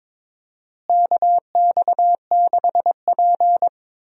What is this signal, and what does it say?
Morse code, heard as a keyed tone, about 21 wpm, 700 Hz. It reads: KX6P
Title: USA Call Signs - 4 Characters - Repeated 3x - 22wpm